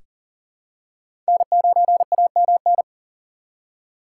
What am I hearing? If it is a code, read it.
Morse code, heard as a keyed tone, about 40 wpm, 700 Hz. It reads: N9AMN
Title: USA Call Signs - 5 Characters - 3x Character Spacing - 40wpm